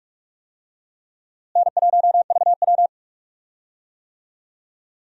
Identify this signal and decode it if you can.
Morse code, heard as a keyed tone, about 45 wpm, 700 Hz. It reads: N1VW
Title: USA Call Signs - 4 Characters - Repeated 2x - 45wpm